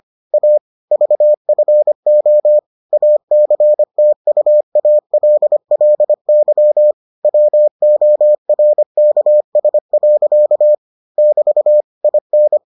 Morse code, heard as a keyed tone, about 25 wpm, 600 Hz. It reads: A VFO ACTUALLY WORKS. <BT> IN